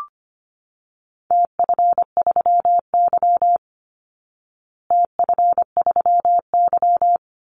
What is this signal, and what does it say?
Morse code, heard as a keyed tone, about 25 wpm, 700 Hz. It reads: TF3Y TF3Y
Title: Call Signs Repeated 3x - 25wpm